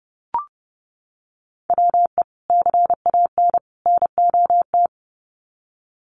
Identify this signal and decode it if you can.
Morse code, heard as a keyed tone, about 30 wpm, 700 Hz. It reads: WE CAN NOT
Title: Sets of 3 Words 30wpm